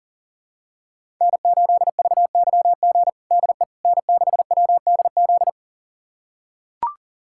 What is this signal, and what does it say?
Morse code, heard as a keyed tone, about 40 wpm, 700 Hz. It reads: N8VYG DE N6WDZ
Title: USA Callsign DE Callsign - 5 Characters - 40wpm